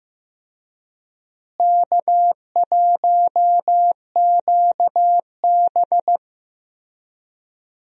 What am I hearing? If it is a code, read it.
Morse code, heard as a keyed tone, about 15 wpm, 700 Hz. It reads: K1QB
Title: USA Call Signs - 4 Characters - Repeated 2x - 15wpm